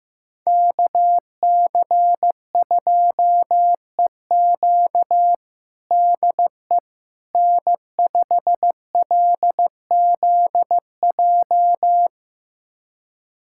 Morse code, heard as a keyed tone, about 15 wpm, 700 Hz. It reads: KC2EQ DE N5LZJ
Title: USA Callsign DE Callsign - 5 Characters - 15wpm